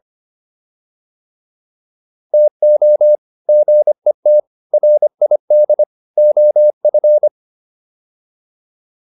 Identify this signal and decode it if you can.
Morse code, heard as a keyed tone, about 25 wpm, 600 Hz. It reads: TO GET RID OF